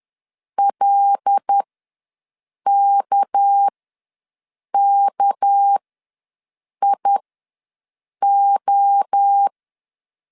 Morse code, heard as a keyed tone, about 10 wpm, spaced as for 6 wpm, 800 Hz. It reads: LKKIO
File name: LEZIONI / 6 PKX0 / 6 PK ws 10wpm 3745